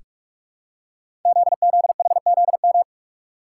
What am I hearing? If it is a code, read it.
Morse code, heard as a keyed tone, about 44 wpm, 700 Hz. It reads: ZZHZM
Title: Code Groups - 5 Letters - 45wpm